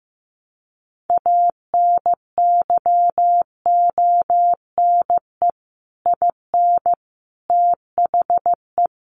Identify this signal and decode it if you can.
Morse code, heard as a keyed tone, about 15 wpm, 700 Hz. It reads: ANYONE IN THE
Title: Sentences from Top 1200 Words 15wpm